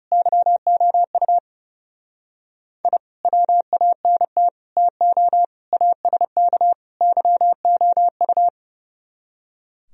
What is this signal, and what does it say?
Morse code, heard as a keyed tone, about 37 wpm, 700 Hz. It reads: YOU I WANT TO ASK YOU
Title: Speed-Racing - Sets of 5 Words 30wpm